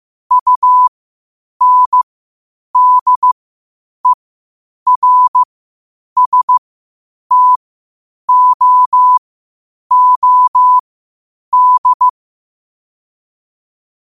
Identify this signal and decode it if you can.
Morse code, heard as a keyed tone, about 15 wpm, spaced as for 8 wpm, 1000 Hz. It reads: UNDERSTOOD
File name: qotdfiles / 8wpm